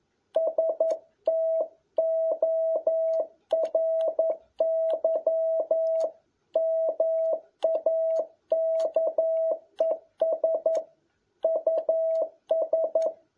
Morse code, heard as a keyed tone, about 11 wpm, 650 Hz. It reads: STORY MAKES US
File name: jh8cbh / 2021cw